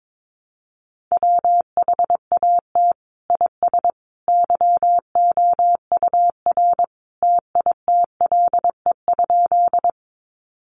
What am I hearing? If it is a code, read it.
Morse code, heard as a keyed tone, about 22 wpm, 700 Hz. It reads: WHAT IS YOUR TITLE?
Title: Sentences from Top 1200 Words 22wpm